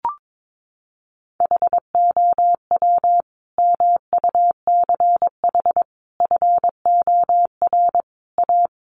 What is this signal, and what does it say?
Morse code, heard as a keyed tone, about 22 wpm, 700 Hz. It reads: HOW MUCH FOR A
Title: Sentences from Top 1500 Words 22wpm